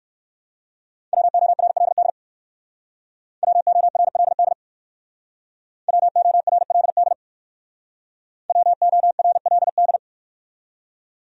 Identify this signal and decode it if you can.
Morse code, heard as a keyed tone, about 59 wpm, 700 Hz. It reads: WORLD WORLD WORLD WORLD